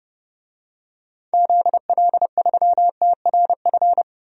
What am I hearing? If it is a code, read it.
Morse code, heard as a keyed tone, about 30 wpm, 700 Hz. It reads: ZL3TRF
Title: DXCC-100 Call Signs - 6 Characters - 30wpm